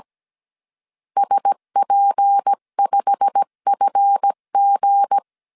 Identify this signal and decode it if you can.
Morse code, heard as a keyed tone, about 17 wpm, 800 Hz. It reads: SP5FG
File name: CW / DET / 1 9 1500C S B 16wpm 2856